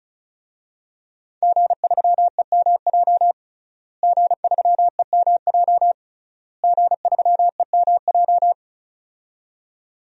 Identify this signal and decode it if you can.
Morse code, heard as a keyed tone, about 36 wpm, 700 Hz. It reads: G3EMJ G3EMJ G3EMJ
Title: DXCC-100 Call Signs - 5 Characters - Repeated 3x - 35wpm